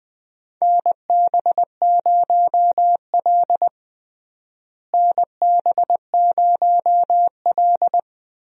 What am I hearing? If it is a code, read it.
Morse code, heard as a keyed tone, about 21 wpm, 700 Hz. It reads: NB0L NB0L
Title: USA Call Signs - 4 Characters - Repeated 2x - 20wpm